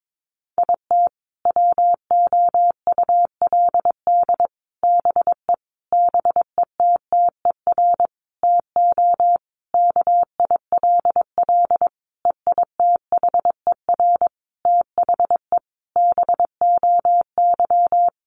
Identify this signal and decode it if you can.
Morse code, heard as a keyed tone, about 22 wpm, 700 Hz. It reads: IT WOULD BE BETTER TO KILL EITHER THE BOY
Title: Sentences from Top 1500 Words 22wpm